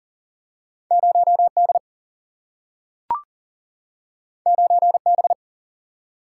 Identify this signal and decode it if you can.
Morse code, heard as a keyed tone, about 40 wpm, 700 Hz. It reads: 0 D 9 B
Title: Single-Letter Single-Number 40wpm